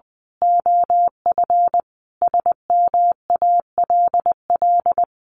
Morse code, heard as a keyed tone, about 20 wpm, 700 Hz. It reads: OF SMALL
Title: A Sound Of Thunder 20wpm